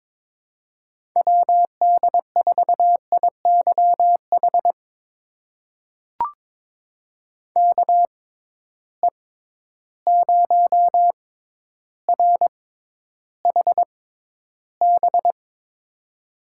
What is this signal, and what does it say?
Morse code, heard as a keyed tone, about 22 wpm, spaced as for 11 wpm, 700 Hz. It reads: WD4IYH KE0RHB
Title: USA Call Signs - 6 Characters - 6x Character Spacing - 22wpm